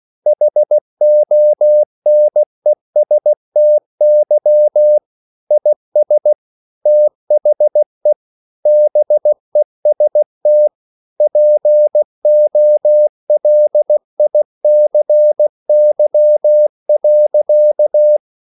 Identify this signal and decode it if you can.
Morse code, heard as a keyed tone, about 16 wpm, 600 Hz. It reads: HONESTY IS THE BEST POLICY.
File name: CW/Practice